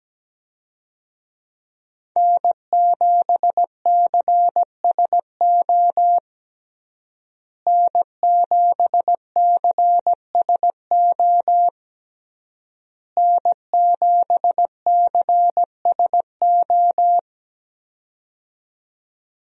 Morse code, heard as a keyed tone, about 17 wpm, 700 Hz. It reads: N7CSO N7CSO N7CSO
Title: USA Call Signs - 5 Characters - Repeated 3x - 17wpm